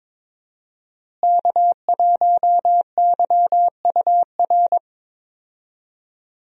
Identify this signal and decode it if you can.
Morse code, heard as a keyed tone, about 22 wpm, 700 Hz. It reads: K1YUR
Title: USA Call Signs - 5 Characters - Repeated 3x - 22wpm